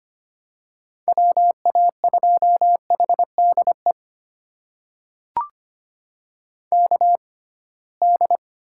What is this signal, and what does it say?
Morse code, heard as a keyed tone, about 25 wpm, spaced as for 19 wpm, 700 Hz. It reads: WA2HDE KD
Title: USA Call Signs - 6 Characters - 6x Character Spacing - 25wpm